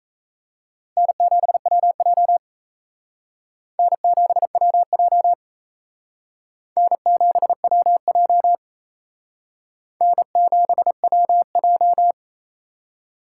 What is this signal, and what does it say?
Morse code, heard as a keyed tone, about 34 wpm, 700 Hz. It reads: N7WJ N7WJ N7WJ N7WJ
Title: Speed Racing - USA 1x2 Call Signs 28wpm